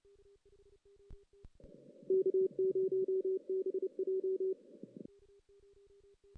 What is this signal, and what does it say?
Morse code, heard as a keyed tone, about 28 wpm, 380 Hz. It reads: K0BJ
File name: 2016 JAN CW NAQP